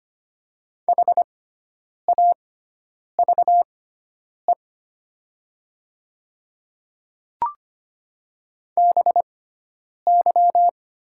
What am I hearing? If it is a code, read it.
Morse code, heard as a keyed tone, about 25 wpm, spaced as for 9 wpm, 700 Hz. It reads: HAVE BY
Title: 6x Character Spacing - Top 100 Words - 25wpm